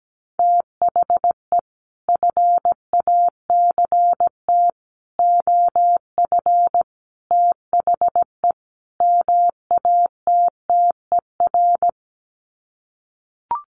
Morse code, heard as a keyed tone, about 17 wpm, 700 Hz. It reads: THE FACT OF THE MATTER
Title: Sets of 5 Words 17wpm